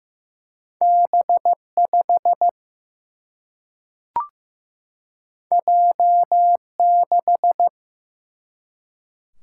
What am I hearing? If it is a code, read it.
Morse code, heard as a keyed tone, about 15 wpm, 700 Hz. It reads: B5 J6
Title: Single-Letter Single-Number 15wpm